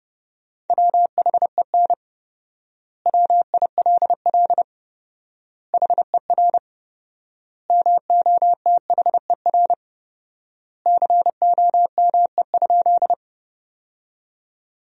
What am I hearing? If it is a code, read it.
Morse code, heard as a keyed tone, about 32 wpm, 700 Hz. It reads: WHEN WILL HER MOTHER COME?
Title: 4x Word-Spacing - Sentences from Top 200 Words - 30wpm